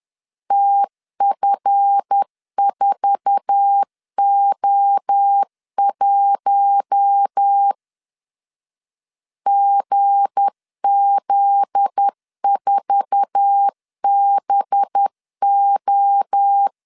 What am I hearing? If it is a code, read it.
Morse code, heard as a keyed tone, about 10 wpm, 800 Hz. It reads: TF4O1 GZ4BO
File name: LEZIONI / 8 YJ45 / 8 YJ4 10wpm 2633